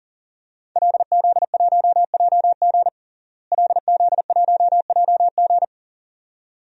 Morse code, heard as a keyed tone, about 40 wpm, 700 Hz. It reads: LZ1JG LZ1JG
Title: DXCC-100 Call Signs - 5 Characters - Repeated 2x - 40wpm